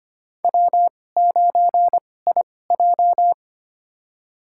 Morse code, heard as a keyed tone, about 25 wpm, spaced as for 18 wpm, 700 Hz. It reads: W9IJ
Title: USA Call Signs - 2x Character Spacing - 25wpm